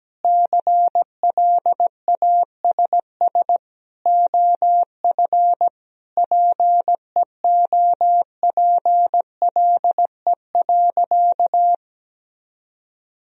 Morse code, heard as a keyed tone, about 17 wpm, 700 Hz. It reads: CLASS OF PEOPLE.
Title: Sentences from Top 1000 Words - Review Part 2 - 17wpm